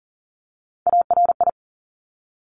Morse code, heard as a keyed tone, about 39 wpm, 700 Hz. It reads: ARI
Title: Top Three Letter Combinations in Words 40wpm